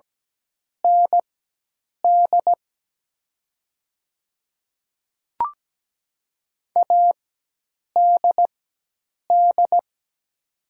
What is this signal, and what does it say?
Morse code, heard as a keyed tone, about 17 wpm, spaced as for 8 wpm, 700 Hz. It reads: ND ADD
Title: 4x Character Spacing - Top 500 Words - 17wpm